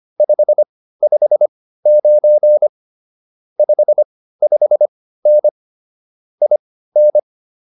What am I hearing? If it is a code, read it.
Morse code, heard as a keyed tone, about 25 wpm, spaced as for 15 wpm, 590 Hz. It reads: 559 55N IN